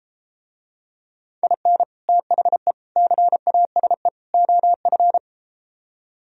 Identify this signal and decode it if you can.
Morse code, heard as a keyed tone, about 33 wpm, 700 Hz. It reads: IN THE CASE OF